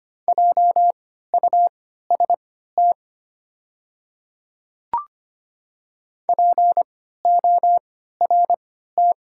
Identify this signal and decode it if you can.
Morse code, heard as a keyed tone, about 25 wpm, spaced as for 14 wpm, 700 Hz. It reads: JUST PORT
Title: 3x Character Spacing - Top 200 Words - 25wpm